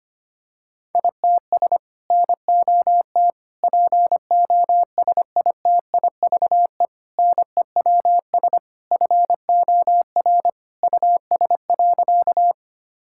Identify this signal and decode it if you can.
Morse code, heard as a keyed tone, about 25 wpm, 700 Hz. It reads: ITS NOT POSITIVE NEWS FOR US.
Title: Sentences from Top 1000 Words - Review Part 5 - 25wpm